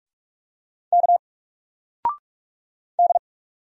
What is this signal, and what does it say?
Morse code, heard as a keyed tone, about 46 wpm, 700 Hz. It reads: K D
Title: Single Letter-Number 45wpm